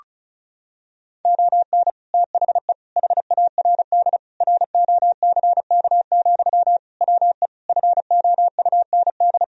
Morse code, heard as a keyed tone, about 36 wpm, 700 Hz. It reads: ON THE HARD ROCK, WE FOUND
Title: Sentences from Top 1000 Words - Review Part 2 - 35wpm